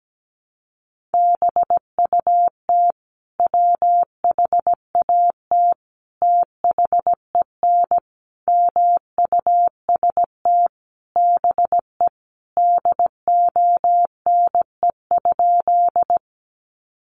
Morse code, heard as a keyed tone, about 17 wpm, 700 Hz. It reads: BUT WHAT THEN MUST BE DONE?
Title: Sentences from Top 500 Words 17wpm